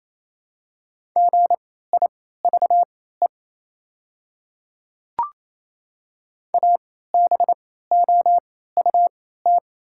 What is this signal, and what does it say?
Morse code, heard as a keyed tone, about 27 wpm, spaced as for 16 wpm, 690 Hz. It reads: GIVE ABOUT
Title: 3x Character Spacing - Top 300 Words - 28wpm